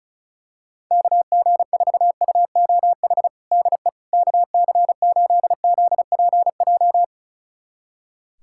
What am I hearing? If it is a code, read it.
Morse code, heard as a keyed tone, about 36 wpm, 700 Hz. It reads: KG4UOH DE KC8ZPJ
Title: USA Callsign DE Callsign - 6 Characters - 35wpm